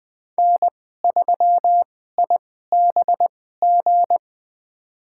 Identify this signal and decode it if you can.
Morse code, heard as a keyed tone, about 20 wpm, spaced as for 15 wpm, 700 Hz. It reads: N3IBG
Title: USA Call Signs - 2x Character Spacing - 20wpm